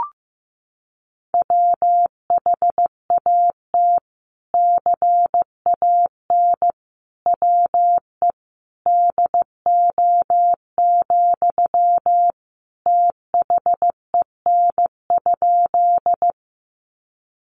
Morse code, heard as a keyed tone, about 15 wpm, 700 Hz. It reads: WHAT CAN WE DO, THEN?
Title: Sentences from Top 100 Words 15wpm